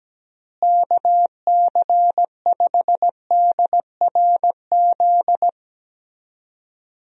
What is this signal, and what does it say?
Morse code, heard as a keyed tone, about 17 wpm, 700 Hz. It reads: KC5DRZ
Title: USA Call Signs - 6 Characters - Repeated 2x - 17wpm